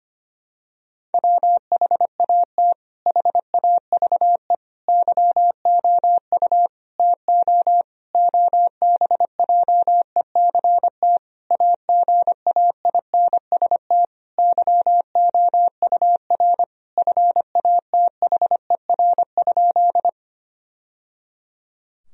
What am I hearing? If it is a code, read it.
Morse code, heard as a keyed tone, about 25 wpm, 700 Hz. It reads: WHAT HAVE YOU TO OBJECT AGAINST YOUR FATHER?
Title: Sentences from Top 1000 Words - Review Part 3 - 25wpm